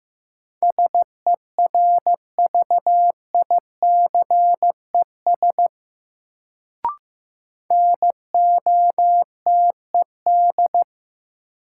Morse code, heard as a keyed tone, about 15 wpm, 700 Hz. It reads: SERVICES NOTED
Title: Sets of 5 Words - New Word Drill - 15wpm